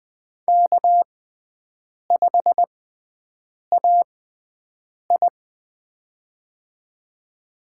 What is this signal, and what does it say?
Morse code, heard as a keyed tone, about 20 wpm, spaced as for 7 wpm, 700 Hz. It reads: K5AI
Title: USA Call Signs - 4 Characters - 6x Character Spacing - 20wpm